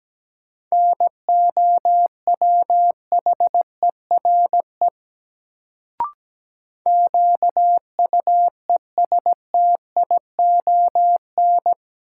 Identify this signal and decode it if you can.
Morse code, heard as a keyed tone, about 17 wpm, 700 Hz. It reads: NOWHERE QUESTION
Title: Sets of 5 Words Encore - New Word Drill - 17wpm